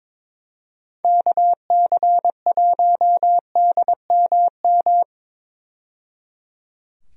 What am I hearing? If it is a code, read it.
Morse code, heard as a keyed tone, about 21 wpm, 700 Hz. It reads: KC1DMM
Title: USA Call Signs - 6 Characters - 22wpm